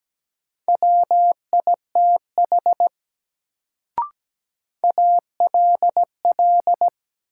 Morse code, heard as a keyed tone, about 17 wpm, 700 Hz. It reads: WITH ALL
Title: Sentences from Top 100 Words - New Word Drill - 17wpm